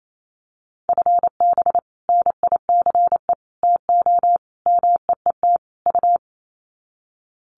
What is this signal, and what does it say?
Morse code, heard as a keyed tone, about 28 wpm, 700 Hz. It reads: FB NICE TO MEET U
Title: Common QSO Phrases 28wpm